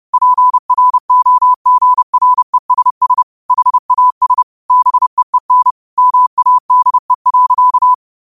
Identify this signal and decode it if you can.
Morse code, heard as a keyed tone, about 30 wpm, 1000 Hz. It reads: PROGRESS HAS BEEN MADE.
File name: qotdfiles / 30wpm